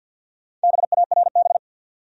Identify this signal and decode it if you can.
Morse code, heard as a keyed tone, about 46 wpm, 700 Hz. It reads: BARB
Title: USA Common Names - Max 4 Letters - 50wpm